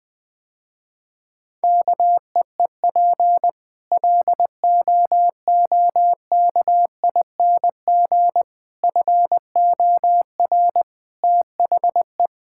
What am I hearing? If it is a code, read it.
Morse code, heard as a keyed tone, about 20 wpm, 700 Hz. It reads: KEEP LOOKING FOR THE
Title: Difficult Sentences with Hints - 20wpm